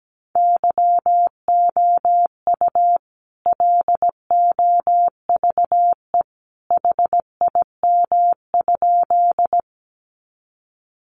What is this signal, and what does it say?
Morse code, heard as a keyed tone, about 17 wpm, 700 Hz. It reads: YOU LOVE HIM?
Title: Sentences from Top 400 Words 17wpm